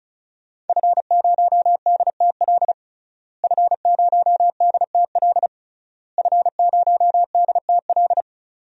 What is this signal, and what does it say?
Morse code, heard as a keyed tone, about 34 wpm, 700 Hz. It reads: F0DTL F0DTL F0DTL
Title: DXCC-100 Call Signs - 5 Characters - Repeated 3x - 35wpm